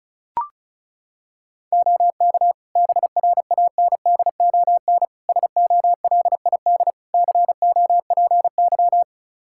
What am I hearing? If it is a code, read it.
Morse code, heard as a keyed tone, about 34 wpm, 700 Hz. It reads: OK BRANDON SOLID COPY
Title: K1USN SST QSO Practice 35wpm